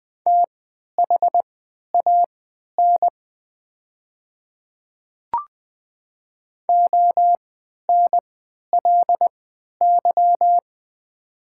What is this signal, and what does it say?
Morse code, heard as a keyed tone, about 20 wpm, spaced as for 11 wpm, 690 Hz. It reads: THAN ONLY
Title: 3x Character Spacing - Top 200 Words - 20wpm